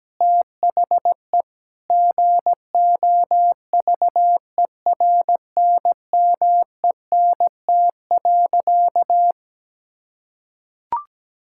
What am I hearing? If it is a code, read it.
Morse code, heard as a keyed tone, about 17 wpm, 700 Hz. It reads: THE GOVERNMENT.
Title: Sentences from Top 1000 Words - Review Part 6 - 17wpm